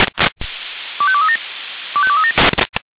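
Five-tone selective call (ZVEI), heard as quick runs of five tones, tones of about 70 ms. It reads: 25237 25237
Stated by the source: ZVEI five-tone selective call received on an SDR.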